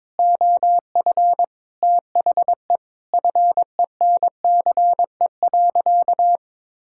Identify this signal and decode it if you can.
Morse code, heard as a keyed tone, about 22 wpm, 700 Hz. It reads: OF THE FENCE.
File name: practice / 22wpm / Sentences with 4-Word Phrases - 22wpm